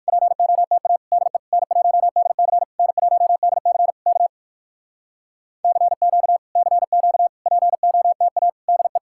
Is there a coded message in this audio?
Yes: Morse code from 0.1 to 9.0 s, heard as a keyed tone, about 59 wpm, 700 Hz. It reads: POTA DE N1DC N1DC K CQ CQ POTA DE